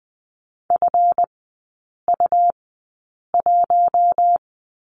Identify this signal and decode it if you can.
Morse code, heard as a keyed tone, about 20 wpm, spaced as for 8 wpm, 700 Hz. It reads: FU1
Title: 138 Lesson 23e - Letter f - Three Characters 20wpm